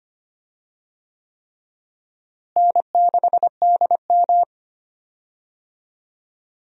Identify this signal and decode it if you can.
Morse code, heard as a keyed tone, about 25 wpm, 700 Hz. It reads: N6DM
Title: USA Call Signs - 4 Characters - Repeated 3x - 25wpm